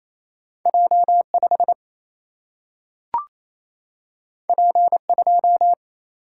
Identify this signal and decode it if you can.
Morse code, heard as a keyed tone, about 28 wpm, 700 Hz. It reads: J 5 P 2
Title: Single-Letter Single-Number 28wpm